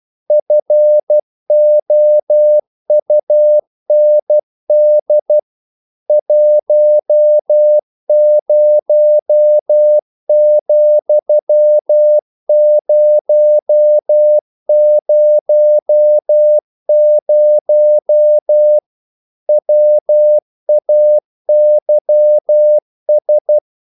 Morse code, heard as a keyed tone, about 12 wpm, 600 Hz. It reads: FOUND 10,000 WAYS
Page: English quotations in CW (telegraphy - morse)